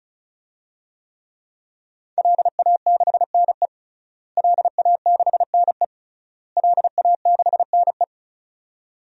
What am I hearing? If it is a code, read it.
Morse code, heard as a keyed tone, about 36 wpm, 700 Hz. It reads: LA6NE LA6NE LA6NE
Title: DXCC-100 Call Signs - 5 Characters - Repeated 3x - 35wpm